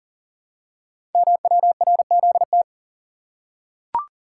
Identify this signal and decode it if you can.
Morse code, heard as a keyed tone, about 39 wpm, 690 Hz. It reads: MWRZT
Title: Code Groups - 5 Letters - 40wpm